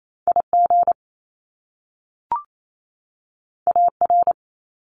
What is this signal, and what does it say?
Morse code, heard as a keyed tone, about 28 wpm, 700 Hz. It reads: IG AR
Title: Top Two Letter Combinations in Words 28wpm